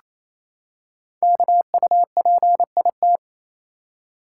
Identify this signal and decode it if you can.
Morse code, heard as a keyed tone, about 27 wpm, 700 Hz. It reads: KUPIT
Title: Code Groups - 5 Letter:Numbers - 28wpm